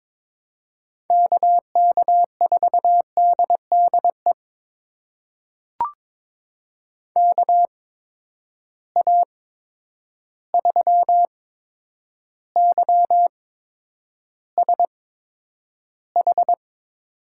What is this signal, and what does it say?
Morse code, heard as a keyed tone, about 22 wpm, spaced as for 9 wpm, 700 Hz. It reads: KK4DDE KA3YSH
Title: USA Call Signs - 6 Characters - 8x Character Spacing - 22wpm